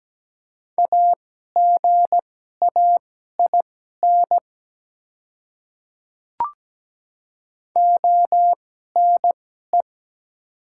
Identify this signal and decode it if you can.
Morse code, heard as a keyed tone, about 17 wpm, spaced as for 12 wpm, 690 Hz. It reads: AGAIN ONE
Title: 2x Character Spacing - Top 300 Words - 17wpm